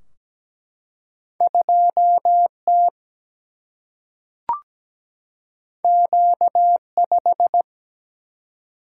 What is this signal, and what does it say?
Morse code, heard as a keyed tone, about 17 wpm, 700 Hz. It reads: 2T Q5
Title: Single-Letter Single-Number 17wpm